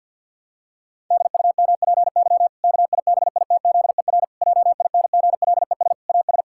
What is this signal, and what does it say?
Morse code, heard as a keyed tone, about 50 wpm, 700 Hz. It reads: DUMPY KIBITZER JINGLES AS
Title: Pangram Challenge v2 - 50wpm